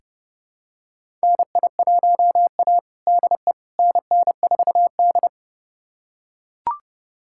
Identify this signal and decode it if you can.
Morse code, heard as a keyed tone, about 30 wpm, 700 Hz. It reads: NI1A DE NN4D
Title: USA Callsign DE Callsign - 4 Characters - 30wpm